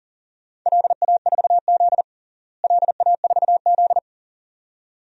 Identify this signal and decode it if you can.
Morse code, heard as a keyed tone, about 37 wpm, 700 Hz. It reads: LA4Z LA4Z
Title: DXCC-100 Call Signs - 4 Characters - Repeated 2x - 40wpm